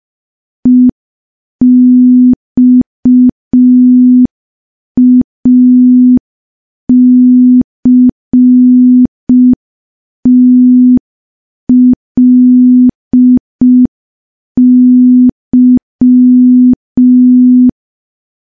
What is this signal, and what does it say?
Morse code, heard as a keyed tone, about 5 wpm, 250 Hz. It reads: EXACTLY
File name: morse / morse-training-05WPM